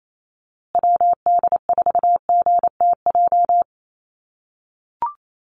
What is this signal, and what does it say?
Morse code, heard as a keyed tone, about 28 wpm, 700 Hz. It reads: WD4GTJ
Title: USA Call Signs Repeated 2x 28wpm